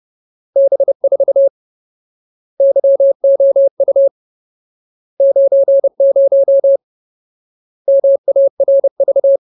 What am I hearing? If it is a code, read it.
Morse code, heard as a keyed tone, about 30 wpm, 550 Hz. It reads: B4 YOU 90 MARV
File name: Ultimate Word List / 30WPM Ultimate